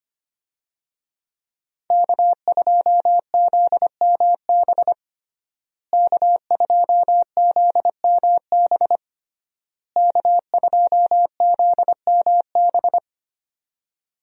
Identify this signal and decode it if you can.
Morse code, heard as a keyed tone, about 25 wpm, 700 Hz. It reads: K2ZMB K2ZMB K2ZMB
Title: USA Call Signs - 5 Characters - Repeated 3x - 25wpm